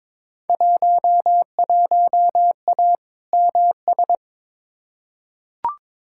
Morse code, heard as a keyed tone, about 22 wpm, 700 Hz. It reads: 11A MS
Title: Field Day Exchanges Encore - 22wpm